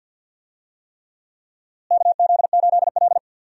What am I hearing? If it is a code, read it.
Morse code, heard as a keyed tone, about 45 wpm, 700 Hz. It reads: KZ8L
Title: USA Call Signs - 4 Characters - Repeated 3x - 50wpm